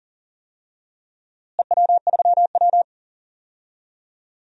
Morse code, heard as a keyed tone, about 38 wpm, 700 Hz. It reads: EW3W
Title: DXCC-100 Call Signs - 4 Characters - Repeated 2x - 40wpm